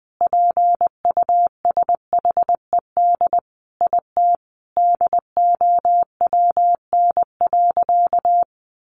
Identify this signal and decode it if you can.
Morse code, heard as a keyed tone, about 20 wpm, 700 Hz. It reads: PUSHED IT DOWN.
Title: A Sound Of Thunder 20wpm